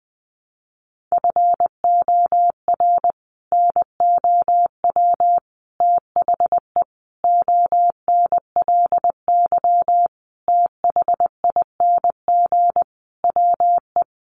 Morse code, heard as a keyed tone, about 20 wpm, 700 Hz. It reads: FOR NOW THE ONLY THING WE
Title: Sentences from Top 200 Words 20wpm